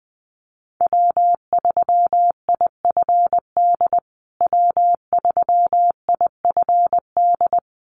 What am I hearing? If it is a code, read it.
Morse code, heard as a keyed tone, about 20 wpm, 700 Hz. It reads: W3IFD W3IFD
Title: USA Call Signs Repeated 2x 20wpm